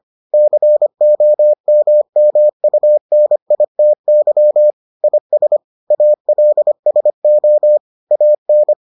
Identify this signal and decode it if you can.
Morse code, heard as a keyed tone, about 25 wpm, 600 Hz. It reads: COMMUNITY IS ALSO AN